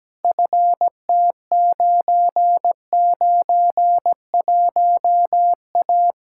Morse code, heard as a keyed tone, about 17 wpm, 700 Hz. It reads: FT991A
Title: Practice QSO - Exchange 2 - Info - 17wpm